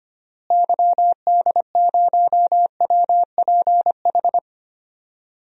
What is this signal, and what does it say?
Morse code, heard as a keyed tone, about 25 wpm, 700 Hz. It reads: YD0WPH
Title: DXCC-100 Call Signs - 6 Characters - Repeated 3x - 25wpm